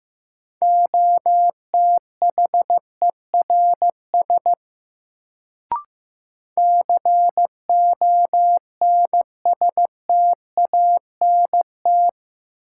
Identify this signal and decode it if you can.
Morse code, heard as a keyed tone, about 15 wpm, 700 Hz. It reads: OTHERS CONSTANT
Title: Sentences from Top 1500 Words - New Word Drill Alt - 15wpm